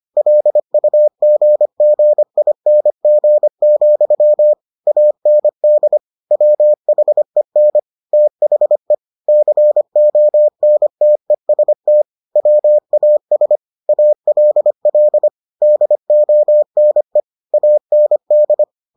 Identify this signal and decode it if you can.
Morse code, heard as a keyed tone, about 25 wpm, 600 Hz. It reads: LUGGING, AND WHEN THE CONTEST WAS ALL DONE AND